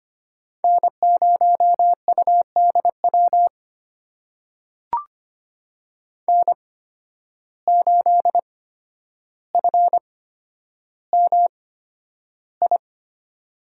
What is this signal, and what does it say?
Morse code, heard as a keyed tone, about 25 wpm, spaced as for 11 wpm, 700 Hz. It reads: N0UDW N8FMI
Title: USA Call Signs - 5 Characters - 8x Character Spacing - 25wpm